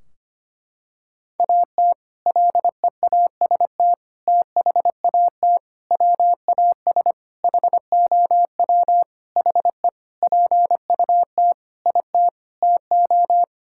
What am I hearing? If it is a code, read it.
Morse code, heard as a keyed tone, about 25 wpm, 700 Hz. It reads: AT LEAST THAT WAS HOW HE PUT IT TO
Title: Sentences from Top 1600 Words 25wpm